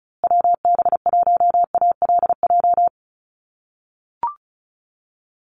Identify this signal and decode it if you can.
Morse code, heard as a keyed tone, about 34 wpm, 700 Hz. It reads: WB1ALJ
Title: USA Call Signs Repeated 2x 35wpm